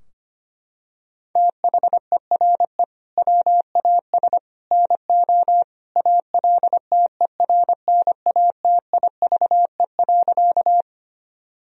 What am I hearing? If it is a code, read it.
Morse code, heard as a keyed tone, about 25 wpm, 700 Hz. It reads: THERE WAS NO ALTERNATIVE.
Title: Sentences from Top 1000 Words - Review Part 4 - 25wpm